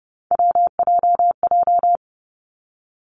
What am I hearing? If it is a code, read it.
Morse code, heard as a keyed tone, about 30 wpm, 700 Hz. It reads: WJJ
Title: Three Challenging Letter-Number Combinations 30wpm